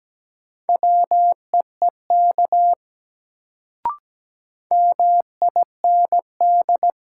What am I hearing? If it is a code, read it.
Morse code, heard as a keyed tone, about 17 wpm, 700 Hz. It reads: WEEK MIND
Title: Sentences from Top 500 Words - New Word Drill - 17wpm